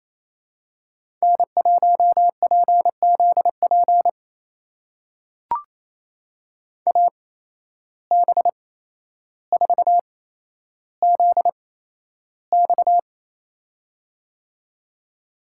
Morse code, heard as a keyed tone, about 28 wpm, spaced as for 12 wpm, 700 Hz. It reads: N1PZP AB4ZX
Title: USA Call Signs - 5 Characters - 8x Character Spacing - 28wpm